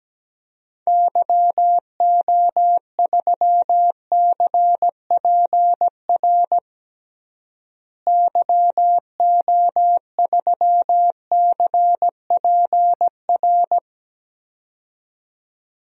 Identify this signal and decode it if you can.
Morse code, heard as a keyed tone, about 17 wpm, 700 Hz. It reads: YO3CPR YO3CPR
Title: DXCC-100 Call Signs - 6 Characters - Repeated 2x - 17wpm